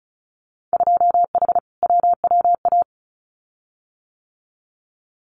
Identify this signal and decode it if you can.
Morse code, heard as a keyed tone, about 34 wpm, 700 Hz. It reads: 2H WWA
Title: Winter Field Day Exchanges 35wpm